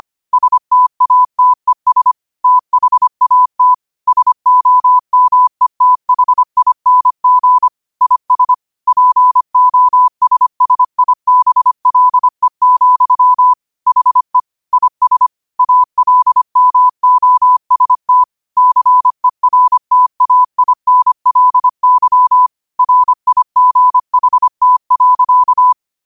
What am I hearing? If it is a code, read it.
Morse code, heard as a keyed tone, about 25 wpm, 1000 Hz. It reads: STATES THAT SOMETHING IS POSSIBLE, HE IS ALMOST CERTAINLY RIGHT.
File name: qotdfiles / 25wpm